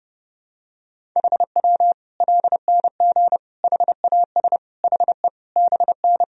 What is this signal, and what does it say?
Morse code, heard as a keyed tone, about 31 wpm, 700 Hz. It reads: HW LNG HAS HE BN
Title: Sentences from Top 100 Words and Abbreviations 30wpm